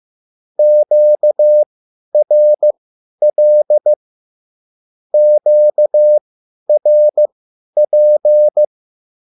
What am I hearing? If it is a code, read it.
Morse code, heard as a keyed tone, about 15 wpm, spaced as for 10 wpm, 600 Hz. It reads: QRL QRP
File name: QSL / CW lesson